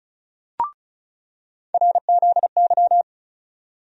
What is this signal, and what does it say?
Morse code, heard as a keyed tone, about 36 wpm, 700 Hz. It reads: RZY
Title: USA Call Sign Postfixes - 3 Letters 35wpm